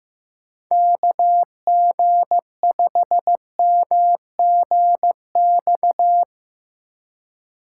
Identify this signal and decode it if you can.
Morse code, heard as a keyed tone, about 15 wpm, 700 Hz. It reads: KG5MGX
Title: USA Call Signs - 6 Characters - 2x Character Spacing - 15wpm